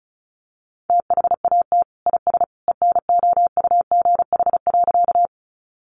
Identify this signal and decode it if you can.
Morse code, heard as a keyed tone, about 34 wpm, 700 Hz. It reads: THAT IS ENOUGH.
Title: Sentences from Top 400 Words 35wpm